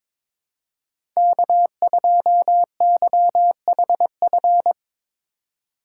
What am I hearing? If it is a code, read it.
Morse code, heard as a keyed tone, about 22 wpm, 700 Hz. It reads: K2YHF
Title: USA Call Signs - 3x Character Spacing - 22wpm